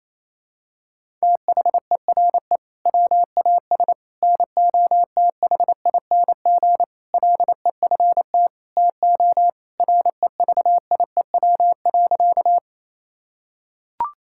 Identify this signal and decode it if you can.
Morse code, heard as a keyed tone, about 28 wpm, 700 Hz. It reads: THERE WAS NOTHING LEFT TO REVIEW.
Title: Sentences from Top 1000 Words - Review Part 6 - 28wpm